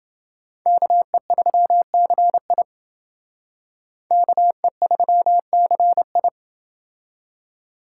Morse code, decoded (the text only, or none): KE3CI KE3CI